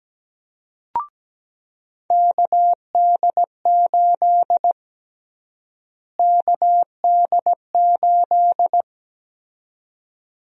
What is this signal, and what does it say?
Morse code, heard as a keyed tone, about 17 wpm, 700 Hz. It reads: KD8 KD8
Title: USA Call Sign Prefixes - Repeated 2x - 17wpm